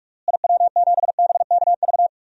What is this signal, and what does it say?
Morse code, heard as a keyed tone, about 46 wpm, 700 Hz. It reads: IW7BKV
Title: DXCC-100 Call Signs - 6 Characters - Repeated 3x - 45wpm